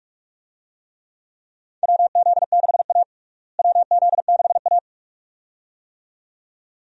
Morse code, heard as a keyed tone, about 43 wpm, 700 Hz. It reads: WZ6A WZ6A
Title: USA Call Signs - 4 Characters - Repeated 2x - 45wpm